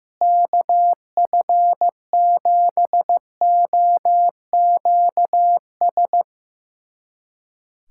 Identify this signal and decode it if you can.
Morse code, heard as a keyed tone, about 15 wpm, 700 Hz. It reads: KF7OQS
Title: USA Callsign DE Callsign 15wpm